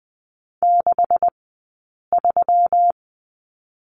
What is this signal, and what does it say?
Morse code, heard as a keyed tone, about 20 wpm, 700 Hz. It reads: 6 3
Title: 158 Lesson 26d - Letter g - Two Characters 20wpm